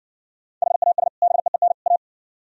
Morse code, heard as a keyed tone, about 64 wpm, 700 Hz. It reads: HAS BEEN A